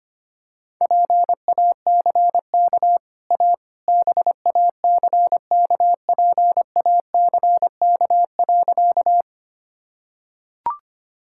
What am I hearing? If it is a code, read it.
Morse code, heard as a keyed tone, about 25 wpm, 700 Hz. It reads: PACK A BACKPACK.
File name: Easy Sentences with Hints - 25wpm